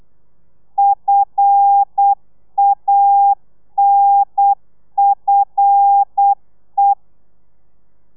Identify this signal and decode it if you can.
Morse code, heard as a keyed tone, about 8 wpm, 800 Hz. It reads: FANFE
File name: telegrafia / i2tfj / Lez5 35